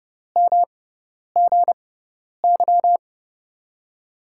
Morse code, heard as a keyed tone, about 31 wpm, spaced as for 10 wpm, 700 Hz. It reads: MGY
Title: USA Call Signs - 5 Characters - 6x Character Spacing - 30wpm